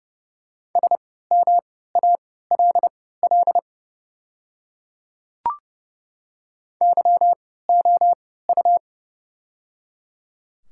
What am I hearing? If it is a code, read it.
Morse code, heard as a keyed tone, about 30 wpm, spaced as for 17 wpm, 700 Hz. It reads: SMALL YOU
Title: 3x Character Spacing - Top 200 Words - 30wpm